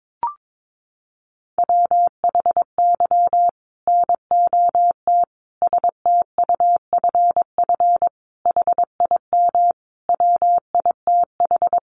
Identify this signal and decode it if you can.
Morse code, heard as a keyed tone, about 22 wpm, 700 Hz. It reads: WHY NOT STUFF HIM WITH
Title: Sentences from Top 1400 Words 22wpm